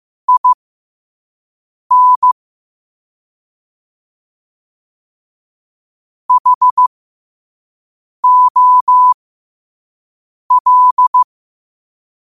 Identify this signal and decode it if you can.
Morse code, heard as a keyed tone, about 15 wpm, spaced as for 5 wpm, 1000 Hz. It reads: IN HOL
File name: qotdfiles / 5wpm